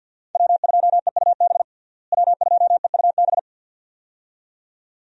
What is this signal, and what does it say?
Morse code, heard as a keyed tone, about 52 wpm, 700 Hz. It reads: W2EUB W2EUB
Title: USA Call Signs - 5 Characters - Repeated 2x - 50wpm